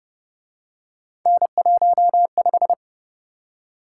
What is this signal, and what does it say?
Morse code, heard as a keyed tone, about 29 wpm, 700 Hz. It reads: N15